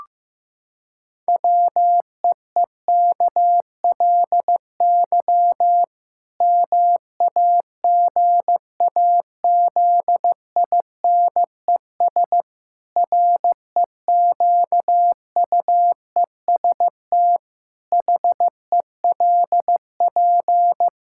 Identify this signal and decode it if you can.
Morse code, heard as a keyed tone, about 15 wpm, 700 Hz. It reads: WEEKLY MAGAZINES REQUEST HELP
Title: Pangram Challenge v2 - 15wpm